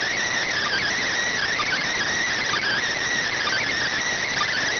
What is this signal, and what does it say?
Emergency Alert System (SAME) tones, heard as harsh buzzing bursts. It is heard from 0.6 to 4.5 s.